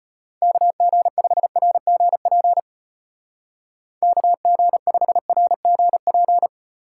Morse code, heard as a keyed tone, about 30 wpm, 700 Hz. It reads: KG5RGP KG5RGP
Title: Speed Racing - USA 2x3 Call Signs 25wpm